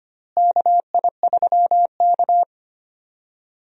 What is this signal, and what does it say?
Morse code, heard as a keyed tone, about 25 wpm, 700 Hz. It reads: KI3K
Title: USA Call Signs - 4 Characters - Repeated 3x - 25wpm